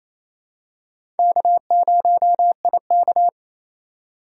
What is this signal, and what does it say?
Morse code, heard as a keyed tone, about 28 wpm, 700 Hz. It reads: K0IK
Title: USA Call Signs - 4x Character Spacing - 28wpm